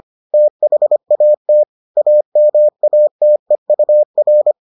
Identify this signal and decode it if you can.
Morse code, heard as a keyed tone, about 25 wpm, 600 Hz. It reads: THAT AMATEUR